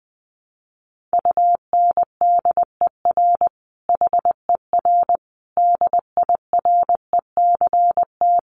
Morse code, heard as a keyed tone, about 20 wpm, 700 Hz. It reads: UNDER HER DIRECT
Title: Sentences from Top 1300 Words 20wpm